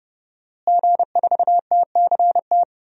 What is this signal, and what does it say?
Morse code, heard as a keyed tone, about 30 wpm, 700 Hz. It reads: G4TCT
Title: DXCC-100 Call Signs - 5 Characters - Repeated 2x - 30wpm